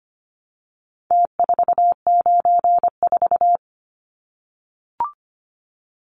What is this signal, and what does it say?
Morse code, heard as a keyed tone, about 25 wpm, 700 Hz. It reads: T494
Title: Numbers - Contest Style - 4-Digit 25wpm